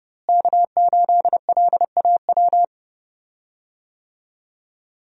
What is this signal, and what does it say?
Morse code, heard as a keyed tone, about 30 wpm, 700 Hz. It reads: K8LAW
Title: USA Call Signs - 5 Characters - Repeated 3x - 30wpm